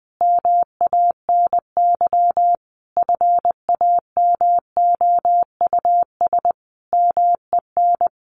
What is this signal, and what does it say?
Morse code, heard as a keyed tone, about 20 wpm, 700 Hz. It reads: MANY FAMOUS MEN
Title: Sentences from Top 1500 Words 20wpm